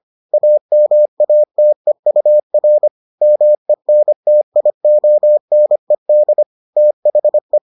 Morse code, heard as a keyed tone, about 25 wpm, 600 Hz. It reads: AMATEUR MENTIONED THE